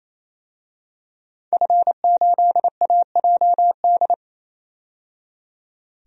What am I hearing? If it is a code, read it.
Morse code, heard as a keyed tone, about 28 wpm, 700 Hz. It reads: F8AJD
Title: DXCC-100 Call Signs - 5 Characters - 28wpm